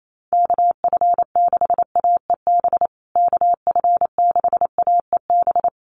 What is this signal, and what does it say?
Morse code, heard as a keyed tone, about 28 wpm, 700 Hz. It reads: KF6AEB KF6AEB
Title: USA Call Signs Repeated 3x 28wpm